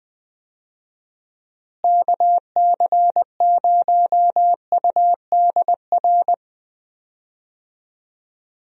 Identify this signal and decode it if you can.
Morse code, heard as a keyed tone, about 20 wpm, 700 Hz. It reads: KC0UDR
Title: USA Call Signs - 6 Characters - Repeated 2x - 20wpm